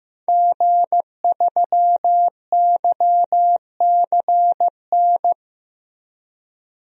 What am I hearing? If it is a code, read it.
Morse code, heard as a keyed tone, about 16 wpm, 700 Hz. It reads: G3YCN
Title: DXCC-100 Call Signs - 5 Characters - 15wpm